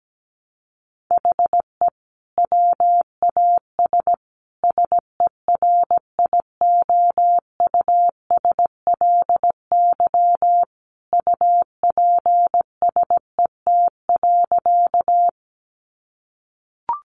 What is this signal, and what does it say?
Morse code, heard as a keyed tone, about 17 wpm, 700 Hz. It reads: HE WAS SERIOUSLY UPSET.
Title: Sentences from Top 1200 Words 17wpm